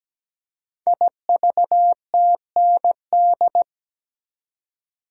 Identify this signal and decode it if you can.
Morse code, heard as a keyed tone, about 17 wpm, 700 Hz. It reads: IVTND
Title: Code Groups - 5 Letters - 17wpm